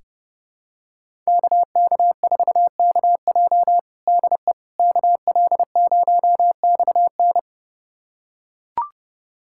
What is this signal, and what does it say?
Morse code, heard as a keyed tone, about 32 wpm, 700 Hz. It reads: KK4KJ DE KL0XN
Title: USA Callsign DE Callsign - 5 Characters - 30wpm